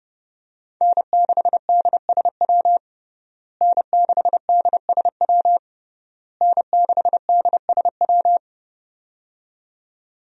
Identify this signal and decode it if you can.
Morse code, heard as a keyed tone, about 30 wpm, 700 Hz. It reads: N6DSW N6DSW N6DSW
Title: USA Call Signs - 5 Characters - Repeated 3x - 30wpm